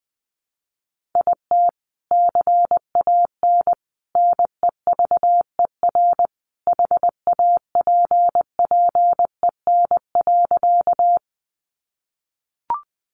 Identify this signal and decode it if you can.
Morse code, heard as a keyed tone, about 20 wpm, 700 Hz. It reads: IT CAN NEVER HAPPEN.
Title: Sentences from Top 400 Words 20wpm